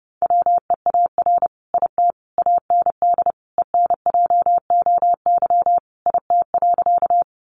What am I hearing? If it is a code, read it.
Morse code, heard as a keyed tone, about 30 wpm, 700 Hz. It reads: WEAR IT AND ENJOY IT.
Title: Sentences from Top 1500 Words 30wpm